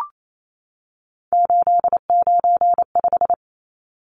Morse code, heard as a keyed tone, about 28 wpm, 700 Hz. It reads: 895
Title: Numbers - 3-Digit 28wpm